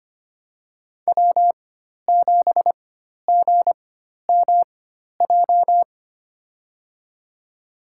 Morse code, heard as a keyed tone, about 25 wpm, spaced as for 12 wpm, 700 Hz. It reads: W7GMJ
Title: USA Call Signs - 5 Characters - 4x Character Spacing - 25wpm